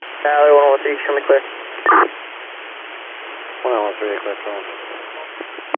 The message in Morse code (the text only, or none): none